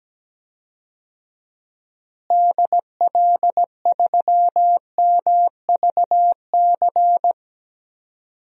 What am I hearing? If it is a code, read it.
Morse code, heard as a keyed tone, about 17 wpm, 700 Hz. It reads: DL3MVC
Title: DXCC-100 Call Signs - 6 Characters - Repeated 3x - 17wpm